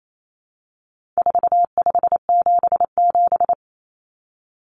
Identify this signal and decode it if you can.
Morse code, heard as a keyed tone, about 28 wpm, 700 Hz. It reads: 4577
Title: Common Cut Numbers - 4-Digit 28wpm